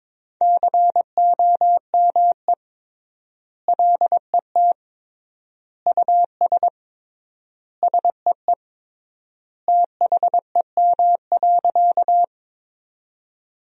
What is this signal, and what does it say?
Morse code, heard as a keyed tone, about 22 wpm, 700 Hz. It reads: COME LET US SEE THEM.
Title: 3x Word-Spacing - Sentences from Top 300 Words - 22wpm